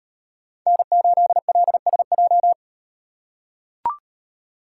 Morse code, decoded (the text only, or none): N8LSJ